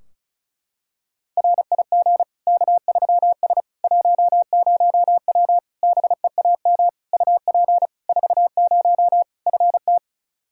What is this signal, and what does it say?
Morse code, heard as a keyed tone, about 36 wpm, 700 Hz. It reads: RIG K3S 10W BEAM UP 40 FT
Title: Practice QSO - Exchange 2 - Info - 35wpm